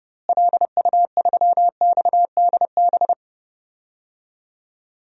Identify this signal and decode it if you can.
Morse code, heard as a keyed tone, about 32 wpm, 700 Hz. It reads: LU3XDB
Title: DXCC-100 Call Signs - 6 Characters - Repeated 3x - 30wpm